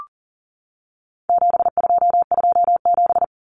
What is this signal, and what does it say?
Morse code, heard as a keyed tone, about 40 wpm, 700 Hz. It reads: 7227
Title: Common Cut Numbers - 4-Digit 40wpm